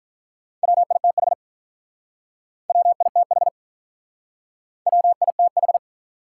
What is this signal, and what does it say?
Morse code, heard as a keyed tone, about 46 wpm, 700 Hz. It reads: WITH WITH WITH